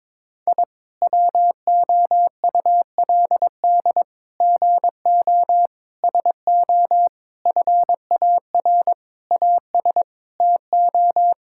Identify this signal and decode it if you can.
Morse code, heard as a keyed tone, about 22 wpm, 700 Hz. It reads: I WOULD GO SO FAR AS TO
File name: practice / 22wpm / Sets of 8 Words - 22wpm